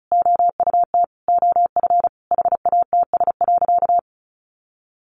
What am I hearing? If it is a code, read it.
Morse code, heard as a keyed tone, about 34 wpm, 700 Hz. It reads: OUT OF HATS.
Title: Easy Phrases 35wpm